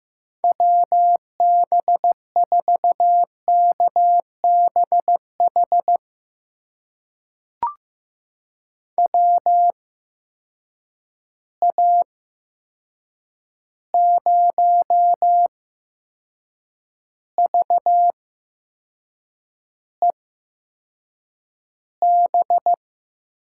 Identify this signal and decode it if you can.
Morse code, heard as a keyed tone, about 15 wpm, spaced as for 6 wpm, 700 Hz. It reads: WB4KBH WA0VEB
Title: USA Call Signs - 6 Characters - 8x Character Spacing - 15wpm